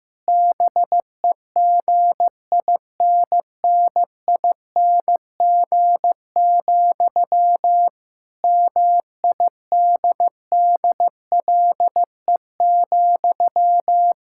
Morse code, heard as a keyed tone, about 15 wpm, 700 Hz. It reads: BEGINNING, MIDDLE,
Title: Sentences from Top 1000 Words - Review Part 6 - 15wpm